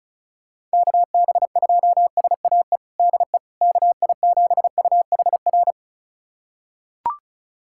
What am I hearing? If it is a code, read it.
Morse code, heard as a keyed tone, about 34 wpm, 700 Hz. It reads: KB2SAE DE KI7UHR
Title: USA Callsign DE Callsign - 6 Characters - 35wpm